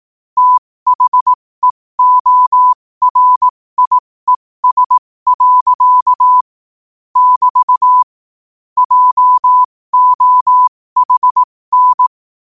Morse code, heard as a keyed tone, about 18 wpm, spaced as for 15 wpm, 1000 Hz. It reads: THEORIES. <BT> JOHN
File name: qotdfiles / 15wpm